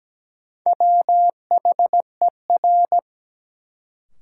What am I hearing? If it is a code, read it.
Morse code, heard as a keyed tone, about 17 wpm, 700 Hz. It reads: WHER
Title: Sentences from Top 1500 Words - New Word Drill - 17wpm